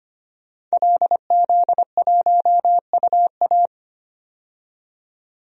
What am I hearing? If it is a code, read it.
Morse code, heard as a keyed tone, about 25 wpm, 700 Hz. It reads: LZ1UA
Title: DXCC-100 Call Signs - 5 Characters - 25wpm